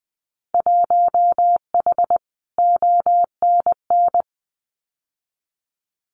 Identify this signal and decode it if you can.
Morse code, heard as a keyed tone, about 20 wpm, 700 Hz. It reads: 1H ONN
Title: Winter Field Day Exchanges 20wpm